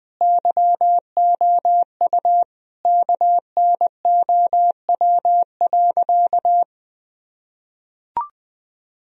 Morse code, heard as a keyed tone, about 20 wpm, 700 Hz. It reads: YOU KNOW.
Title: Sentences from Top 1000 Words - Review Part 2 - 20wpm